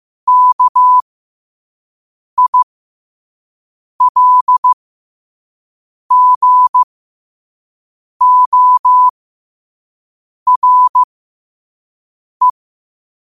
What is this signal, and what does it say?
Morse code, heard as a keyed tone, about 15 wpm, spaced as for 5 wpm, 1000 Hz. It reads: KILGORE
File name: qotdfiles / 5wpm